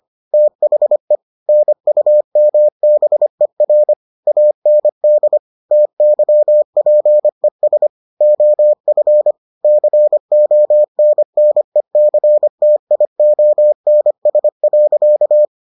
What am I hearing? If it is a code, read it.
Morse code, heard as a keyed tone, about 25 wpm, 600 Hz. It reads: THE NUMBER AND TYPES OF CONNECTIONS.